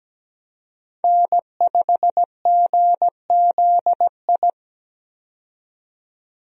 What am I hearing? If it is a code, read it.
Morse code, heard as a keyed tone, about 17 wpm, 700 Hz. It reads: N5GZI
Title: USA Call Signs - 5 Characters - 17wpm